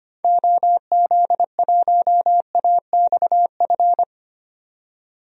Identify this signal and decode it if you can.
Morse code, heard as a keyed tone, about 25 wpm, 700 Hz. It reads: OZ1AXF
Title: DXCC-100 Call Signs - 6 Characters - 25wpm